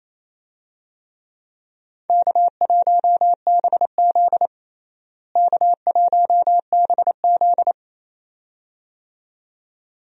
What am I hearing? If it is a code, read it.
Morse code, heard as a keyed tone, about 28 wpm, 700 Hz. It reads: K1BZ K1BZ
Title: USA Call Signs - 4 Characters - Repeated 2x - 28wpm